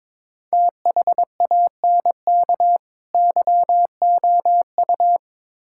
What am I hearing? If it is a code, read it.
Morse code, heard as a keyed tone, about 22 wpm, 700 Hz. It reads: THANK YOU
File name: Build A Phrase - 5-Word Phrases - 22wpm